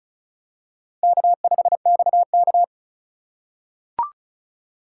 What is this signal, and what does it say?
Morse code, heard as a keyed tone, about 36 wpm, 700 Hz. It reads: K5XK
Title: K1USN SST QSO Practice - 35wpm